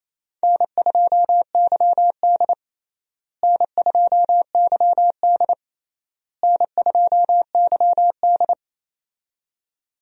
Morse code, heard as a keyed tone, about 27 wpm, 700 Hz. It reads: N2YD N2YD N2YD
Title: USA Call Signs - 4 Characters - Repeated 3x - 28wpm